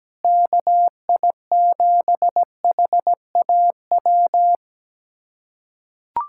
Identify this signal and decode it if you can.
Morse code, heard as a keyed tone, about 17 wpm, 700 Hz. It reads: KI7HAW
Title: USA Callsign DE Callsign - 6 Characters - 17wpm